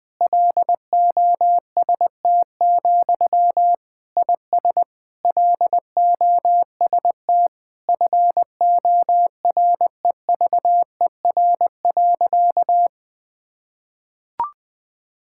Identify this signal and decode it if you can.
Morse code, heard as a keyed tone, about 20 wpm, 700 Hz. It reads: LOST, IS LOST FOREVER.
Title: Sentences from Top 1600 Words 20wpm